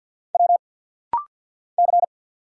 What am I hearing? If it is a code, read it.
Morse code, heard as a keyed tone, about 49 wpm, 700 Hz. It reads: W C
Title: Single Letters - Rapid-Fire 50wpm